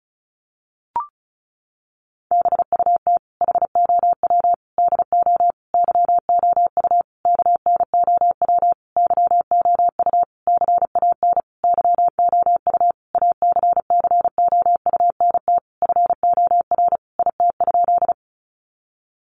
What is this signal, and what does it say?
Morse code, heard as a keyed tone, about 34 wpm, 700 Hz. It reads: BUT HOW DO YOU KNOW YOU CAN YOU ACCOUNT FOR IT?
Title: Sentences from Top 900 Words 35wpm